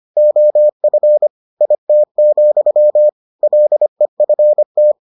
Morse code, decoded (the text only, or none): OF IT, LEFT